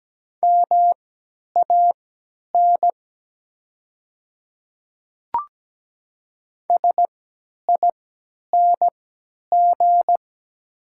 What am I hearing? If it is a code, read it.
Morse code, heard as a keyed tone, about 17 wpm, spaced as for 10 wpm, 700 Hz. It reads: MAN SING
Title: 3x Character Spacing - Top 500 Words - 17wpm